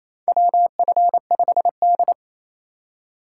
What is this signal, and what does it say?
Morse code, heard as a keyed tone, about 28 wpm, 700 Hz. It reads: WF5D
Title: USA Call Signs - 4 Characters - 2x Character Spacing - 28wpm